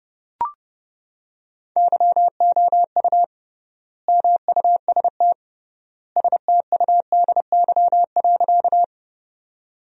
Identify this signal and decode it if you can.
Morse code, heard as a keyed tone, about 30 wpm, 700 Hz. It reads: YOU MUST STUDY.
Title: 3x Word-Spacing - Sentences from Top 300 Words - 30wpm